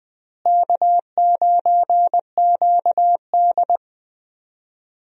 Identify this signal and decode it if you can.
Morse code, heard as a keyed tone, about 20 wpm, 700 Hz. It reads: K9QD
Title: USA Call Signs - 4 Characters - Repeated 2x - 20wpm